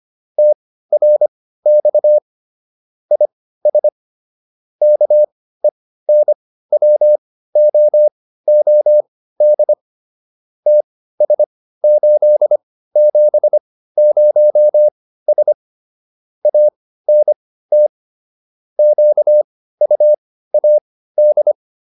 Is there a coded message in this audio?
Yes: Morse code from 0.4 to 21.5 s, heard as a keyed tone, about 25 wpm, spaced as for 15 wpm, 600 Hz. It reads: TRX IS KENWOOD TS870S ANT QUAD